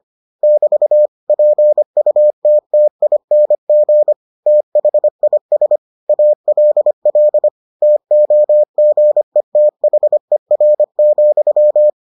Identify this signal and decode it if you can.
Morse code, heard as a keyed tone, about 25 wpm, 600 Hz. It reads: <BT> PUTTING THIS ALL TOGETHER,